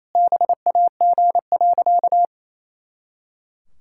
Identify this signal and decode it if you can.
Morse code, heard as a keyed tone, about 28 wpm, 700 Hz. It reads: BAG.
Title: Pangram Challenge v2 - 28wpm